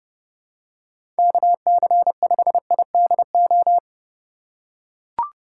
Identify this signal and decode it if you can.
Morse code, heard as a keyed tone, about 30 wpm, 700 Hz. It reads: KC5IDO
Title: USA Call Signs - 6 Characters - 2x Character Spacing - 30wpm